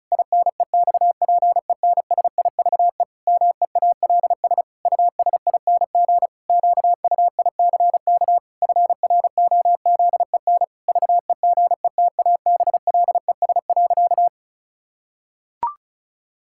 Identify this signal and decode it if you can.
Morse code, heard as a keyed tone, about 36 wpm, 700 Hz. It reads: INEXPENSIVE MEALS USING QUICK FROZEN VEGETABLES.
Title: Pangram Challenge v2 - 35wpm